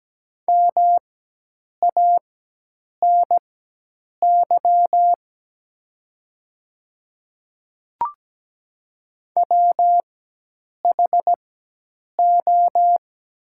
Morse code, heard as a keyed tone, about 17 wpm, spaced as for 8 wpm, 700 Hz. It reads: MANY WHO
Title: 4x Character Spacing - Top 200 Words - 17wpm